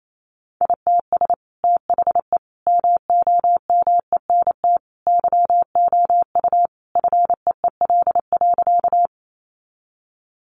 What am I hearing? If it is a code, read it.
Morse code, heard as a keyed tone, about 28 wpm, 700 Hz. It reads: ITS THE MOMENT YOU FEEL.
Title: Sentences from Top 1000 Words 28wpm